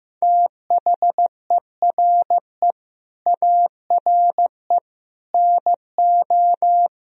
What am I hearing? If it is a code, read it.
Morse code, heard as a keyed tone, about 16 wpm, 700 Hz. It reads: THERE ARE NO
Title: Speed-Racing - Sets of 3 Words 15wpm